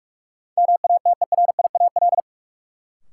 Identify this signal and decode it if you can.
Morse code, heard as a keyed tone, about 42 wpm, 700 Hz. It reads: MATERIAL
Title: Sentences from Top 1000 Words - Review Part 5 - New Word Drill - 45wpm